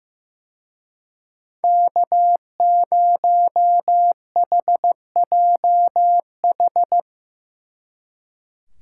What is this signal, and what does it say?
Morse code, heard as a keyed tone, about 17 wpm, 700 Hz. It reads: K0HJH
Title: Speed Racing - USA 1x3 Call Signs 15wpm